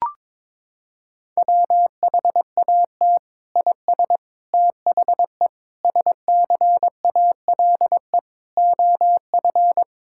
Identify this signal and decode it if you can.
Morse code, heard as a keyed tone, about 22 wpm, 700 Hz. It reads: WHAT IS THE SCALE OF
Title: Sentences from Top 1000 Words - Review Part 5 - 22wpm